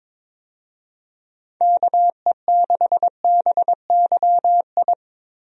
Morse code, heard as a keyed tone, about 22 wpm, 700 Hz. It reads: KE6BYI
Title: USA Call Signs - 6 Characters - Repeated 3x - 22wpm